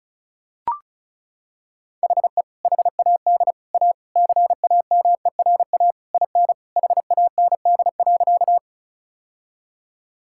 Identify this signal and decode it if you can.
Morse code, heard as a keyed tone, about 34 wpm, 700 Hz. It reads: HE HAD A CAMERA IN HAND.
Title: Sentences from Top 1000 Words - Review Part 5 - 35wpm